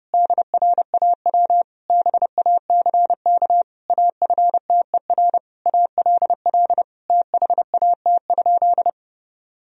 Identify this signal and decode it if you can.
Morse code, heard as a keyed tone, about 30 wpm, 700 Hz. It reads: DRAW BACK AFTER ALL THAT?
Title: Sentences from Top 1000 Words - Review Part 1 - 30wpm